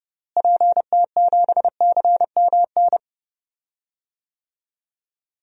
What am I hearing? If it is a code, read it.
Morse code, heard as a keyed tone, about 30 wpm, 700 Hz. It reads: PT7CMN
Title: DXCC-100 Call Signs - 6 Characters - Repeated 2x - 30wpm